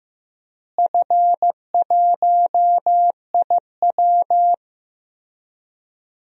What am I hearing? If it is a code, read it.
Morse code, heard as a keyed tone, about 16 wpm, 700 Hz. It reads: F1IW
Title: DXCC-100 Call Signs - 4 Characters - Repeated 3x - 15wpm